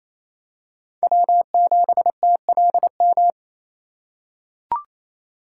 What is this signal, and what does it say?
Morse code, heard as a keyed tone, about 28 wpm, 700 Hz. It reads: W7TLM
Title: USA Call Signs - 5 Characters - 4x Character Spacing - 28wpm